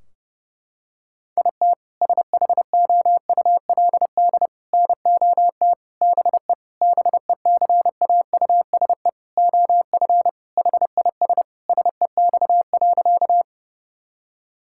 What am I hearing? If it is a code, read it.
Morse code, heard as a keyed tone, about 28 wpm, 700 Hz. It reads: IT SHOULD NOT BE BECAUSE OF HIS SEX.
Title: Sentences from Top 1600 Words 30wpm